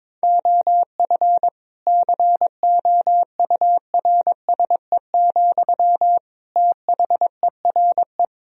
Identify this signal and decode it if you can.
Morse code, heard as a keyed tone, about 22 wpm, 700 Hz. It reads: OF COURSE, THERE
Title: Sentences from Top 1000 Words - Review Part 4 - 22wpm